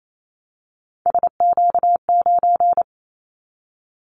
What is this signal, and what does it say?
Morse code, heard as a keyed tone, about 28 wpm, 700 Hz. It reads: SQ9
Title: Callsign Prefixes for Top 25 DXCC - Repeated 3x - 28wpm